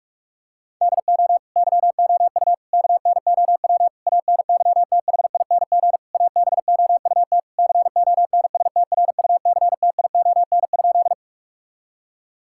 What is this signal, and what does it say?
Morse code, heard as a keyed tone, about 46 wpm, 700 Hz. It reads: DO YOU KNOW ANYTHING ABOUT CONSTRUCTION?
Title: Sentences from Top 1600 Words 45wpm